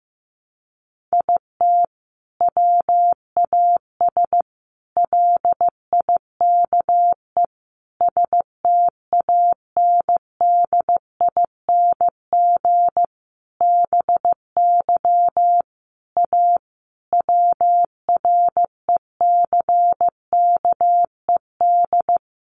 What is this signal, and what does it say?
Morse code, heard as a keyed tone, about 15 wpm, 700 Hz. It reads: IT WAS LIKE STANDING BY A WRECKED
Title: A Sound Of Thunder 15wpm